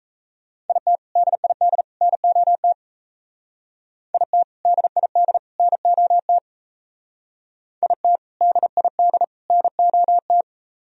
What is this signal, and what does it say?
Morse code, heard as a keyed tone, about 33 wpm, 700 Hz. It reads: IT DID NOT IT DID NOT IT DID NOT
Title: Speed-Racing - Sets of 3 Words 28wpm